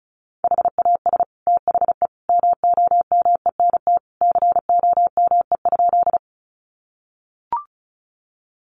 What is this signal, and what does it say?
Morse code, heard as a keyed tone, about 34 wpm, 700 Hz. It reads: HAS THE MOMENT COME?
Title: Sentences from Top 1000 Words 35wpm